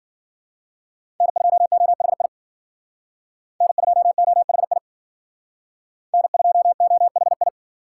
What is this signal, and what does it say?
Morse code, heard as a keyed tone, about 51 wpm, 700 Hz. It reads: N2OHI N2OHI N2OHI